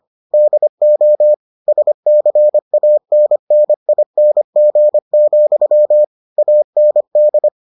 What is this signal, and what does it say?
Morse code, heard as a keyed tone, about 25 wpm, 600 Hz. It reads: DO SCANNING, AND